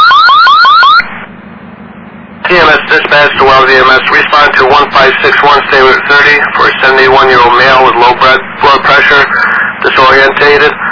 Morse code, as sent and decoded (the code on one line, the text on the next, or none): none